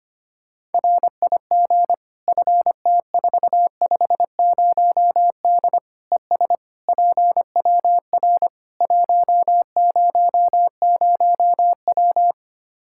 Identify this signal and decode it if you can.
Morse code, heard as a keyed tone, about 25 wpm, 700 Hz. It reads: RIG FT450D ES PWR 100W
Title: Practice QSO - Exchange 2 - Info - 25wpm